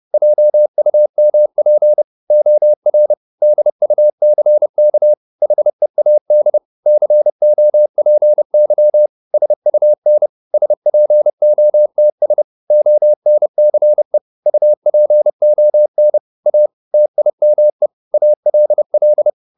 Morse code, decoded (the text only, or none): JUMP OR DUCK HEAD COPY SUN SPOTS ONCE UPON A TIME ALL